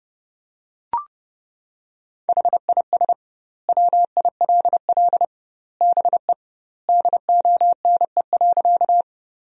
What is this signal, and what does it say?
Morse code, heard as a keyed tone, about 30 wpm, 700 Hz. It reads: HIS WILL BE DONE.
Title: 2x Word-Spacing - Sentences from Top 500 Words - 30wpm